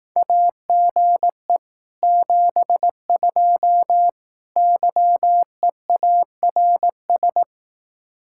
Morse code, heard as a keyed tone, about 18 wpm, 700 Hz. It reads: AGE 72 YEARS
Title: Speed-Racing Common QSO Phrases Encore - 15wpm